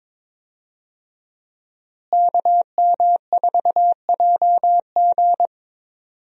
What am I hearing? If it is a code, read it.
Morse code, heard as a keyed tone, about 22 wpm, 700 Hz. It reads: KM4JG
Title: USA Call Signs - 5 Characters - Repeated 2x - 22wpm